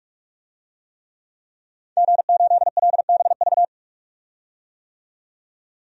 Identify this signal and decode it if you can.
Morse code, heard as a keyed tone, about 43 wpm, 700 Hz. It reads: G8LBV
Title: DXCC-100 Call Signs - 5 Characters - 45wpm